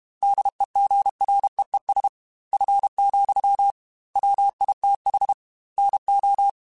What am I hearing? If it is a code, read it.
Morse code, heard as a keyed tone, about 32 wpm, 800 Hz. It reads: DEGREES F, WITH NO
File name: Weather Sets / Weather Sets-mp3s / Weather 3 17 @30